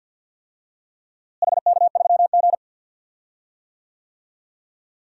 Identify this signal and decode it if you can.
Morse code, heard as a keyed tone, about 51 wpm, 700 Hz. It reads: HK3G
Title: DXCC-100 Call Signs - 4 Characters - Repeated 3x - 50wpm